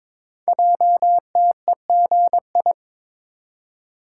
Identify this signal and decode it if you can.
Morse code, heard as a keyed tone, about 22 wpm, 700 Hz. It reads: JTEGI
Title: Code Groups - 5 Letters - 22wpm